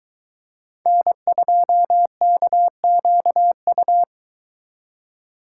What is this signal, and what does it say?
Morse code, heard as a keyed tone, about 23 wpm, 700 Hz. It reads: N2KQU